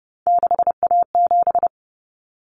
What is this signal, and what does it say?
Morse code, heard as a keyed tone, about 30 wpm, 700 Hz. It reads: 6A7
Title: Common Cut Numbers - 3-Digit 30wpm